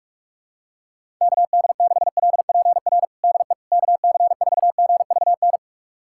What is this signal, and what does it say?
Morse code, heard as a keyed tone, about 46 wpm, 700 Hz. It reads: KD6LPR DE KC4GVN
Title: USA Callsign DE Callsign - 6 Characters - 45wpm